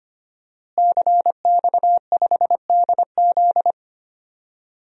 Morse code, heard as a keyed tone, about 25 wpm, 700 Hz. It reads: CX5DZ
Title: DXCC-100 Call Signs - 5 Characters - Repeated 3x - 25wpm